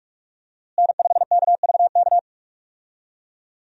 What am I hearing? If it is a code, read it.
Morse code, heard as a keyed tone, about 46 wpm, 700 Hz. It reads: N5KVK
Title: USA Call Signs - 5 Characters - Repeated 3x - 45wpm